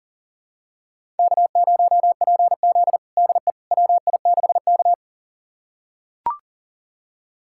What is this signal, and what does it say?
Morse code, heard as a keyed tone, about 37 wpm, 700 Hz. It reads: K0PZ DE WI6K
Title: USA Callsign DE Callsign - 4 Characters - 40wpm